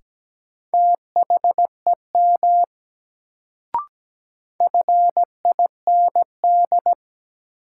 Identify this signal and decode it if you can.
Morse code, heard as a keyed tone, about 17 wpm, 700 Hz. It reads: THEM FIND
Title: Sentences from Top 100 Words - New Word Drill - 17wpm